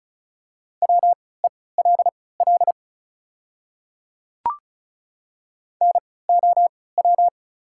Morse code, heard as a keyed tone, about 36 wpm, spaced as for 20 wpm, 690 Hz. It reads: WELL NOW
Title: 3x Character Spacing - Top 200 Words - 35wpm